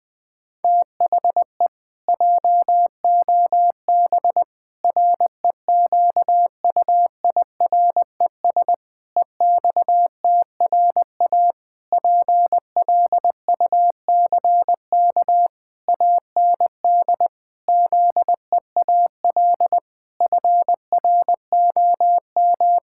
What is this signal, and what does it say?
Morse code, heard as a keyed tone, about 21 wpm, 700 Hz. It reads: THE JOB REQUIRES EXTRA PLUCK AND ZEAL FROM
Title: Pangram Challenge v2 - 20wpm